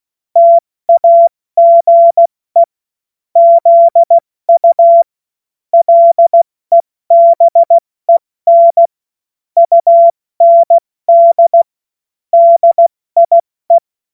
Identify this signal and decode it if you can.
Morse code, heard as a keyed tone, about 16 wpm, 680 Hz. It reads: TAGE ZU LEBEN UND DIE
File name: Morsekurs / Witze / Witz 13. Juni 2023 70 F80